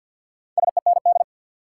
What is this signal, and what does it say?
Morse code, heard as a keyed tone, about 53 wpm, 700 Hz. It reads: SEND
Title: Sentences from Top 1500 Words - New Word Drill - 50wpm